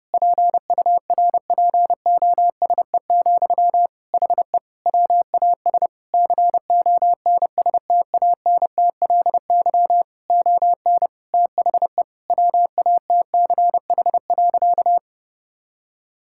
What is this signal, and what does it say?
Morse code, heard as a keyed tone, about 30 wpm, 700 Hz. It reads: PURPOSE, HE WAS CONSTANTLY ON THE WATCH.
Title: Sentences from Top 1600 Words 30wpm